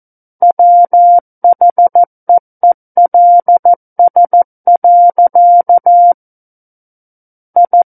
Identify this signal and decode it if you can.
Morse code, heard as a keyed tone, about 14 wpm, 700 Hz. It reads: WHEELS. I
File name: Book One / 05-Chapter 5 - 14 WPM